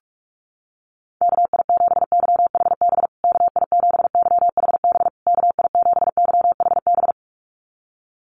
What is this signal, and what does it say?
Morse code, heard as a keyed tone, about 44 wpm, 690 Hz. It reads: KI7YHB KI7YHB KI7YHB
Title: USA Call Signs Repeated 3x 45wpm